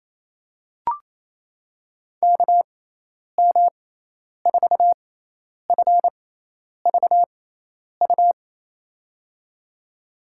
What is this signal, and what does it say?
Morse code, heard as a keyed tone, about 28 wpm, spaced as for 10 wpm, 700 Hz. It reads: KM4FVU
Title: USA Call Signs - 6x Character Spacing - 28wpm